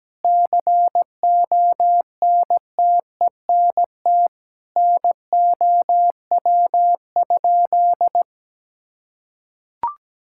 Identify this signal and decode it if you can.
Morse code, heard as a keyed tone, about 17 wpm, 700 Hz. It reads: CONTENT NOW?
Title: Sentences from Top 1000 Words - Review Part 6 - 17wpm